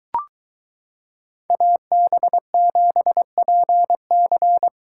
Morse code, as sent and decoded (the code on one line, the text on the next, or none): .- -... --... .--. -.-.
AB7PC